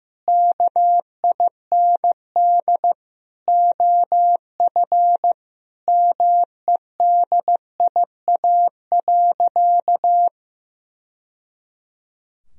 Morse code, heard as a keyed tone, about 16 wpm, 700 Hz. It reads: KIND OF MEDIA.
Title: Sentences from Top 1000 Words - Review Part 4 - 15wpm